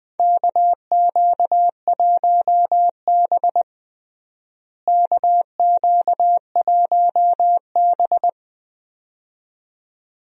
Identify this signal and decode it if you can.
Morse code, heard as a keyed tone, about 20 wpm, 700 Hz. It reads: KQ1B KQ1B
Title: USA Call Signs - 4 Characters - Repeated 2x - 20wpm